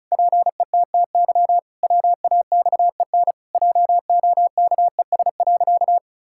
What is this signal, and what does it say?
Morse code, heard as a keyed tone, about 36 wpm, 700 Hz. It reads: PETTY WAXEN JOKES.
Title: Pangram Challenge v2 - 35wpm